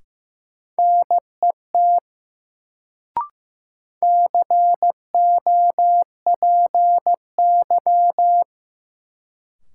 Morse code, heard as a keyed tone, about 15 wpm, 700 Hz. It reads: NET COPY
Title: Sentences from Top 1400 Words - New Word Drill Alt - 15wpm